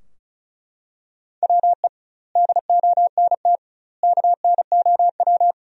Morse code, heard as a keyed tone, about 36 wpm, 700 Hz. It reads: WE DONT KNOW
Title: 2x Word-Spacing - Sentences from Top 400 Words - 35wpm